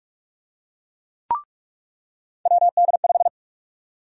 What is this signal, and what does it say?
Morse code, heard as a keyed tone, about 48 wpm, 700 Hz. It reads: WD5
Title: USA Call Sign Prefixes - 45wpm